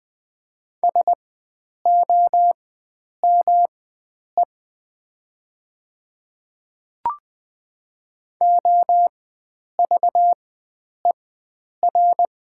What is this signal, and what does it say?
Morse code, heard as a keyed tone, about 20 wpm, spaced as for 9 wpm, 700 Hz. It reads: SOME OVER
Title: 4x Character Spacing - Top 100 Words - 20wpm